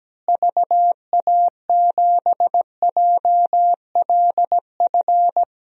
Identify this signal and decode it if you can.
Morse code, heard as a keyed tone, about 17 wpm, 700 Hz. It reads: VA7JLF
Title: DXCC-100 Call Signs - 6 Characters - Repeated 2x - 17wpm